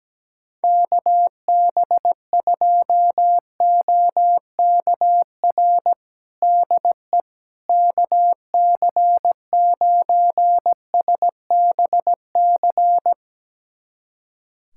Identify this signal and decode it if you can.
Morse code, heard as a keyed tone, about 17 wpm, 700 Hz. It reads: KB2OKR DE KC9SBC
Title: USA Callsign DE Callsign - 6 Characters - 17wpm